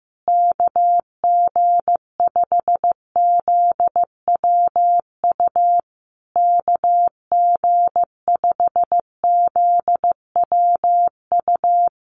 Morse code, heard as a keyed tone, about 15 wpm, 700 Hz. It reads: KG5ZWU KG5ZWU
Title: USA Call Signs Repeated 3x 15wpm